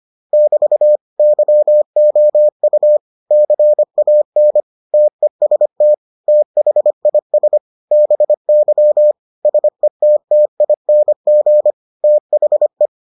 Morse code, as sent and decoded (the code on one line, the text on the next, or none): -...- / -.-- --- ..- / -.-. .- -. / - . ... - / - .... .. ... / -... -.-- / ... - - .. -. --. / - .... .
<BT> YOU CAN TEST THIS BY SETTING THE